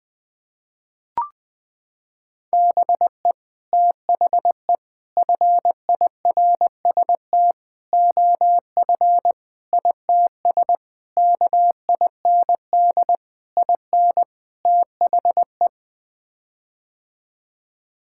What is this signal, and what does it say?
Morse code, heard as a keyed tone, about 20 wpm, 700 Hz. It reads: BE THE FIRST OF ITS KIND IN THE
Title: Sets of 8 Words - 20wpm